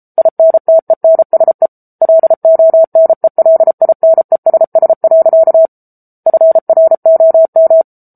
Morse code, heard as a keyed tone, about 34 wpm, 650 Hz. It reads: INTENSE LONELINESS. FROM
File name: Book Two / 07-Chapter 7 - 33 WPM